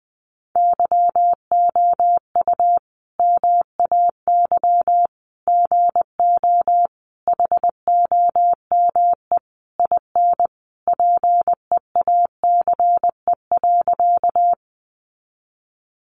Morse code, heard as a keyed tone, about 20 wpm, 700 Hz. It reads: YOU MAY GO HOME IN PEACE.
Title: Sentences from Top 1500 Words 20wpm